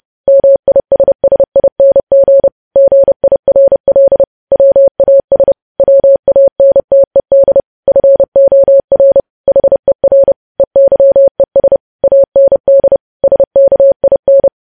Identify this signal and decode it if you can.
Morse code, heard as a keyed tone, about 30 wpm, 550 Hz. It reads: MISSING GIRL WAS WANTED FOR HER EYES AND SKIN
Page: news headlines in morse code at 30 wpm